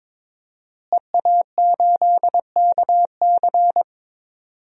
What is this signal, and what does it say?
Morse code, heard as a keyed tone, about 22 wpm, 700 Hz. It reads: EA8KC
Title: DXCC-100 Call Signs - 5 Characters - Repeated 2x - 22wpm